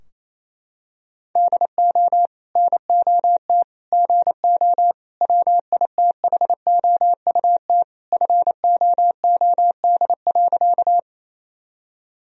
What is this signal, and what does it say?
Morse code, heard as a keyed tone, about 27 wpm, 700 Hz. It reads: DO NOT GO WITHOUT FOOD.
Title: Sentences from Top 1000 Words - Review Part 3 - 28wpm